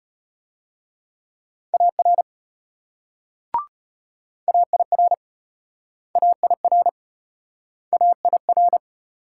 Morse code, heard as a keyed tone, about 33 wpm, 700 Hz. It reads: AR AIR AIR AIR